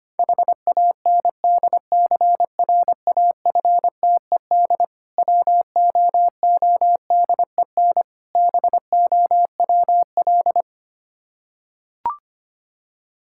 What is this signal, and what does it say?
Morse code, heard as a keyed tone, about 25 wpm, 690 Hz. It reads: HANDCRAFTED WOODEN BOWL
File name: practice / 25wpm / Difficult Words with Hints - 25wpm